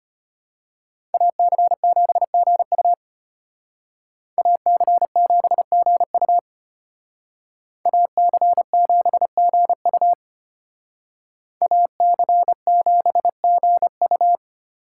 Morse code, heard as a keyed tone, about 32 wpm, 700 Hz. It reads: AC7GU AC7GU AC7GU AC7GU